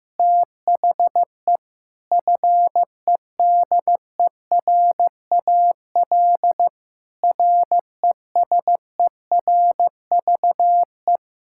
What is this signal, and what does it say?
Morse code, heard as a keyed tone, about 15 wpm, 700 Hz. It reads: THE FEDERAL RESERVE
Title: Sentences from Top 1000 Words - Review Part 6 - 15wpm